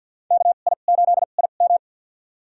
Morse code, heard as a keyed tone, about 51 wpm, spaced as for 36 wpm, 700 Hz. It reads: KI8IM
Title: USA Call Signs - 5 Characters - 2x Character Spacing - 50wpm